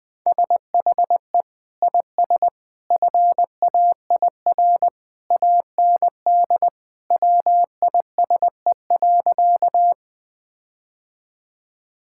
Morse code, heard as a keyed tone, about 20 wpm, 700 Hz. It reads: SHE IS FAIR AND WISE.
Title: Sentences from Top 1600 Words 20wpm